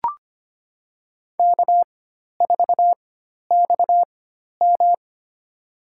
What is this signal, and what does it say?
Morse code, heard as a keyed tone, about 25 wpm, spaced as for 12 wpm, 690 Hz. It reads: K4XM
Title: USA Call Signs - 4 Characters - 4x Character Spacing - 25wpm